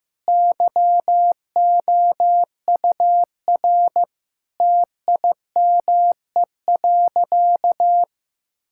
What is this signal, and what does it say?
Morse code, heard as a keyed tone, about 15 wpm, 700 Hz. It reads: YOUR TIME.
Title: Sentences from Top 1600 Words 15wpm